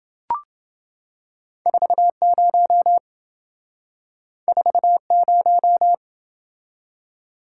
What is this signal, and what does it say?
Morse code, heard as a keyed tone, about 27 wpm, 700 Hz. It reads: 40 40